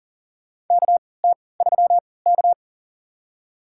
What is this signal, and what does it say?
Morse code, heard as a keyed tone, about 40 wpm, spaced as for 23 wpm, 700 Hz. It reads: KT3K
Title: USA Call Signs - 4 Characters - 3x Character Spacing - 40wpm